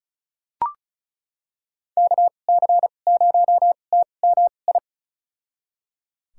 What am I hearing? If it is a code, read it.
Morse code, heard as a keyed tone, about 36 wpm, spaced as for 25 wpm, 700 Hz. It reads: KC0TMI
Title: USA Call Signs - 6 Characters - 2x Character Spacing - 35wpm